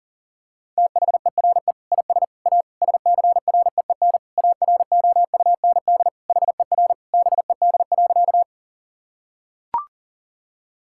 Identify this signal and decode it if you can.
Morse code, heard as a keyed tone, about 39 wpm, 700 Hz. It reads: THERE IS A SCREEN AROUND HER BED.
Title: Sentences from Top 1000 Words - Review Part 6 - 40wpm